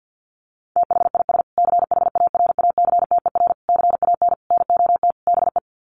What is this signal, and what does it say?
Morse code, heard as a keyed tone, about 50 wpm, 700 Hz. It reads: THIS CHARACTER CAN NOT BE
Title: Sets of 5 Words Encore2 50wpm